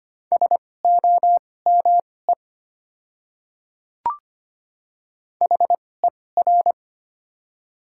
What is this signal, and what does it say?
Morse code, heard as a keyed tone, about 25 wpm, spaced as for 18 wpm, 700 Hz. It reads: SOME HER
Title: 2x Character Spacing - Top 100 Words - 25wpm